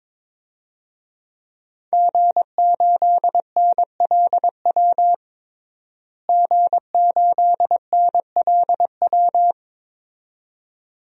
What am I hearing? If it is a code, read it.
Morse code, heard as a keyed tone, about 22 wpm, 700 Hz. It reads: G8NLW G8NLW
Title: DXCC-100 Call Signs - 5 Characters - Repeated 2x - 22wpm